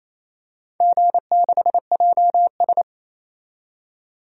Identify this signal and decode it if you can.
Morse code, heard as a keyed tone, about 28 wpm, 700 Hz. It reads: G6JS
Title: DXCC-100 Call Signs - 4 Characters - 28wpm